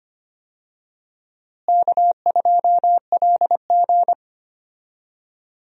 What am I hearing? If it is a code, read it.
Morse code, heard as a keyed tone, about 25 wpm, 700 Hz. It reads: K2LG
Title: USA Call Signs - 4 Characters - Repeated 2x - 25wpm